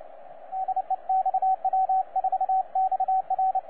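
Morse code, heard as a keyed tone, about 28 wpm, 760 Hz, with radio noise. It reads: DE XW4XR
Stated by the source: XW4XR heard on 12 m CW, from DX World.net.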